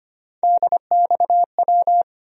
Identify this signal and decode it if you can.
Morse code, heard as a keyed tone, about 25 wpm, 700 Hz. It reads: DXW
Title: USA Call Sign Postfixes - 3 Letters 25wpm